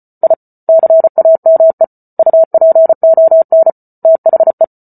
Morse code, heard as a keyed tone, about 34 wpm, 650 Hz. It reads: I CAME UPON THE
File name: Book Two / 08-Chapter 8 - 34 WPM